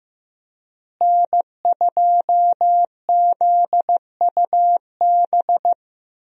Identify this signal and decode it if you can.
Morse code, heard as a keyed tone, about 15 wpm, 700 Hz. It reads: N2ZUB
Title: USA Call Signs - 5 Characters - Repeated 3x - 15wpm